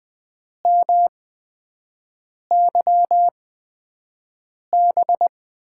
Morse code, heard as a keyed tone, about 20 wpm, spaced as for 5 wpm, 700 Hz. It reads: MYB
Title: USA Call Signs - 6 Characters - 8x Character Spacing - 20wpm